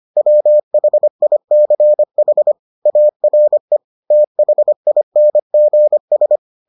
Morse code, heard as a keyed tone, about 25 wpm, 600 Hz. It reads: WHICH ARE THINGS